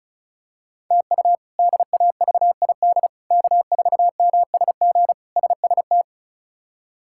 Morse code, heard as a keyed tone, about 36 wpm, 700 Hz. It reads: TU DAVID K4MSG SST
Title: K1USN SST QSO Practice - 35wpm